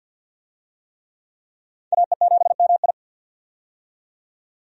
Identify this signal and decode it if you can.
Morse code, heard as a keyed tone, about 49 wpm, 700 Hz. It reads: AE7MI
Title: USA Call Signs - 5 Characters - 50wpm